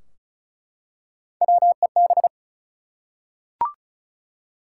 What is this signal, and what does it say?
Morse code, heard as a keyed tone, about 36 wpm, 700 Hz. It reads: WEB
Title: Three Letter Words - 35wpm